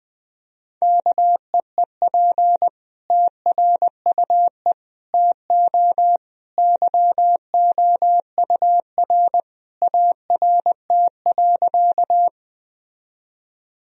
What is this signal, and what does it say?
Morse code, heard as a keyed tone, about 21 wpm, 700 Hz. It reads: KEEP TRUE TO YOUR ART.
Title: Sentences from Top 1000 Words - Review Part 3 - 20wpm